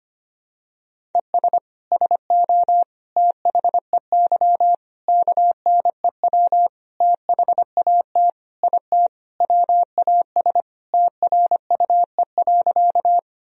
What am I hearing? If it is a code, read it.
Morse code, heard as a keyed tone, about 25 wpm, 700 Hz. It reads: ES SO THEY KNEW THAT IT WAS TRUE.
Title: Sentences from Top 400 Words and Abbreviations 25wpm